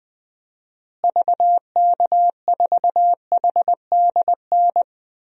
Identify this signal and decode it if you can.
Morse code, heard as a keyed tone, about 20 wpm, 700 Hz. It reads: VK4HDN
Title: DXCC-100 Call Signs - 6 Characters - Repeated 3x - 20wpm